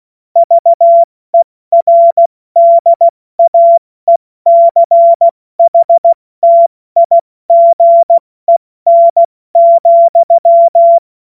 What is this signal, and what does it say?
Morse code, heard as a keyed tone, about 16 wpm, spaced as for 14 wpm, 680 Hz. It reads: VERDAECHTIGEN,
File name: Morsekurs/Witze/Witz 12 Dezember 2023 70 F80